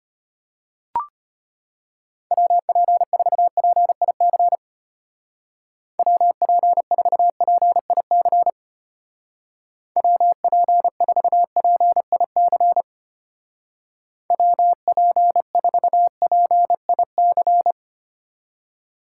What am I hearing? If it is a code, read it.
Morse code, heard as a keyed tone, about 31 wpm, 700 Hz. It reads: WP4PIC WP4PIC WP4PIC WP4PIC
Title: Speed Racing - USA 2x3 Call Signs 25wpm